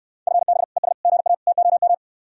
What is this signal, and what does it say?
Morse code, heard as a keyed tone, about 69 wpm, 700 Hz. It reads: FB ES GA TOM